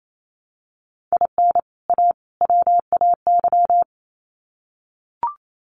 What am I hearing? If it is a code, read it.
Morse code, heard as a keyed tone, about 28 wpm, 700 Hz. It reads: IN A WAY
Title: Sets of 3 Words 28wpm